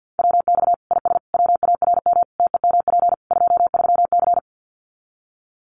Morse code, heard as a keyed tone, about 48 wpm, 700 Hz. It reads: WX IS WARM TEMP 23C
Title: Common QSO Phrases 50wpm